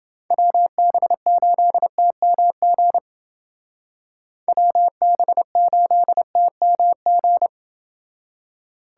Morse code, heard as a keyed tone, about 29 wpm, 700 Hz. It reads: WB8TMG WB8TMG